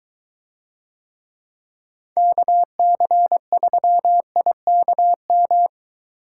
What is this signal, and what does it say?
Morse code, heard as a keyed tone, about 21 wpm, 700 Hz. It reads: KC3IKM